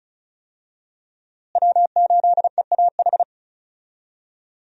Morse code, heard as a keyed tone, about 36 wpm, 700 Hz. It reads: W8EAH
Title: USA Call Signs - 5 Characters - Repeated 2x - 35wpm